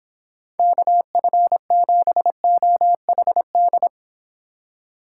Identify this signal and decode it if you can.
Morse code, heard as a keyed tone, about 26 wpm, 700 Hz. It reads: KF7OHD